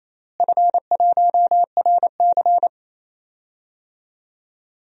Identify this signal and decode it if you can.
Morse code, heard as a keyed tone, about 28 wpm, 700 Hz. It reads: F1RC
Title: DXCC-100 Call Signs - 4 Characters - Repeated 3x - 28wpm